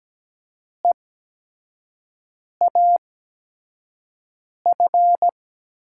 Morse code, heard as a keyed tone, about 17 wpm, spaced as for 5 wpm, 700 Hz. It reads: EAF
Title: USA Call Signs - 6 Characters - 8x Character Spacing - 17wpm